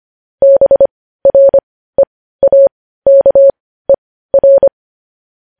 Morse code, heard as a keyed tone, about 25 wpm, spaced as for 15 wpm, 550 Hz. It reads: BREAKER
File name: podcast / News headlines-1x-25x15